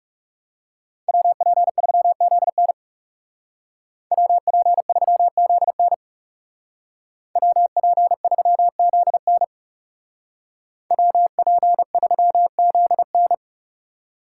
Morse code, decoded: WP3ZN WP3ZN WP3ZN WP3ZN

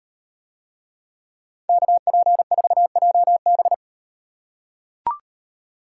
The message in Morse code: KP4JB

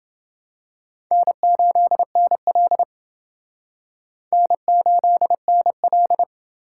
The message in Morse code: N8NL N8NL

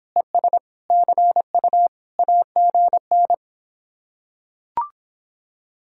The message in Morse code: ES CU AGN

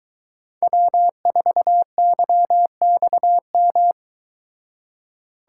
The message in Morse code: W4YXM

